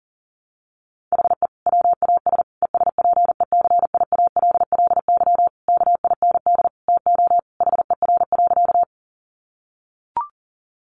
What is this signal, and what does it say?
Morse code, heard as a keyed tone, about 40 wpm, 700 Hz. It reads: HE WAS ESPECIALLY KIND TO HER.
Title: Sentences from Top 700 Words 40wpm